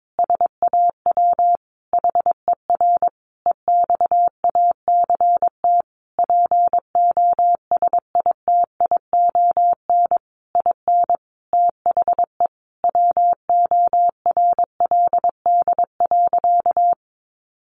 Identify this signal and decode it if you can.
Morse code, heard as a keyed tone, about 22 wpm, 700 Hz. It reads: SAW HER EXACT POSITION IN THE WORLD.
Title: Sentences from Top 1300 Words 22wpm